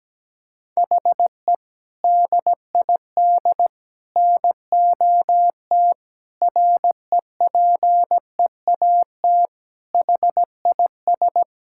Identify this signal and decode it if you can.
Morse code, heard as a keyed tone, about 17 wpm, 700 Hz. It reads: HE DID NOT REPEAT HIS
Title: Sentences from Top 1600 Words 17wpm